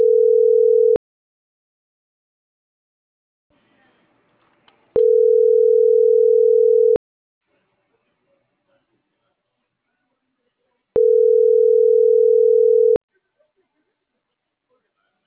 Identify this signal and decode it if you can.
Ringback tone (North America), 440 + 480 Hz.